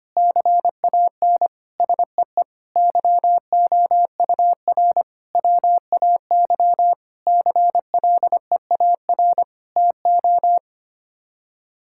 Morse code, heard as a keyed tone, about 25 wpm, 700 Hz. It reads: CAN SEE YOUR WAY CLEAR TO
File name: Sets of 9 Words - 25wpm